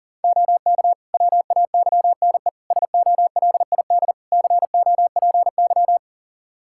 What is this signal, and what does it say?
Morse code, heard as a keyed tone, about 40 wpm, 700 Hz. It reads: OK WAYNE SOLID COPY
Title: Practice QSO - Exchange 3 - More Info - 40wpm